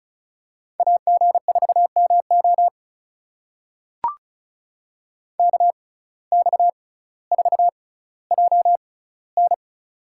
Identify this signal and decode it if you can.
Morse code, heard as a keyed tone, about 36 wpm, spaced as for 18 wpm, 700 Hz. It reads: AG4MO KX4JN
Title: USA Call Signs - 5 Characters - 6x Character Spacing - 35wpm